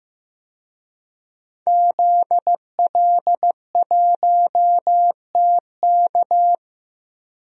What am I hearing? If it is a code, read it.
Morse code, heard as a keyed tone, about 15 wpm, 700 Hz. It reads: ZL1TK
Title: DXCC-100 Call Signs - 5 Characters - 15wpm